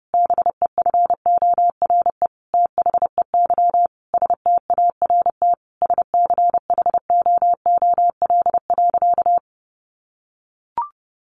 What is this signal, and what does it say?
Morse code, heard as a keyed tone, about 30 wpm, 700 Hz. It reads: BEFORE THEY START SCHOOL.
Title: Sentences from Top 1100 Words 30wpm